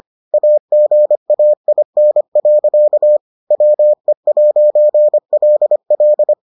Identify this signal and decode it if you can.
Morse code, heard as a keyed tone, about 25 wpm, 600 Hz. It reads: AGAIN. WE'LL